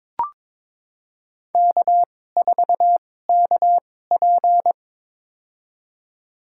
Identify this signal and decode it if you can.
Morse code, heard as a keyed tone, about 22 wpm, spaced as for 16 wpm, 700 Hz. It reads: K4KP
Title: USA Call Signs - 4 Characters - 2x Character Spacing - 22wpm